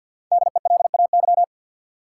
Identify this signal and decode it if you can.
Morse code, heard as a keyed tone, about 52 wpm, 700 Hz. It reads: DELAY
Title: Five Letter Words - 50wpm